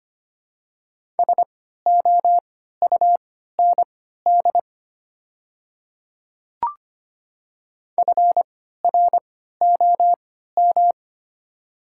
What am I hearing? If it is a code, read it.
Morse code, heard as a keyed tone, about 25 wpm, spaced as for 14 wpm, 700 Hz. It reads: SOUND FROM
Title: 3x Character Spacing - Top 200 Words - 25wpm